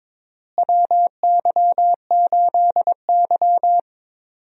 Morse code, heard as a keyed tone, about 23 wpm, 700 Hz. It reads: WY8Y
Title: USA Call Signs - 4 Characters - Repeated 3x - 22wpm